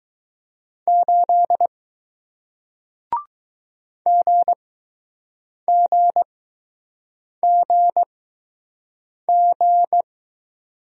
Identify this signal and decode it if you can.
Morse code, heard as a keyed tone, about 20 wpm, spaced as for 6 wpm, 700 Hz. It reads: 8 GGGG